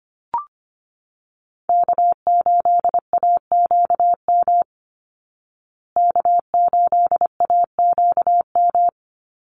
Morse code, heard as a keyed tone, about 25 wpm, 700 Hz. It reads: K8AQM K8AQM
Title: Call Signs Repeated 3x - 25wpm